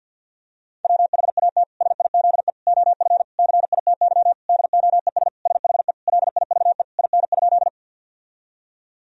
Morse code, heard as a keyed tone, about 52 wpm, 700 Hz. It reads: WHAT SIZE OF CITY DOES SHE LIVE IN?
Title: Sentences from Top 1000 Words - Review Part 2 - 50wpm